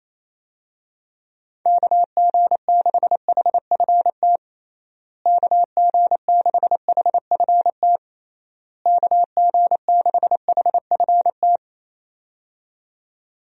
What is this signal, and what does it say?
Morse code, heard as a keyed tone, about 28 wpm, 700 Hz. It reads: KG6HFT KG6HFT KG6HFT
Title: USA Call Signs - 6 Characters - Repeated 3x - 28wpm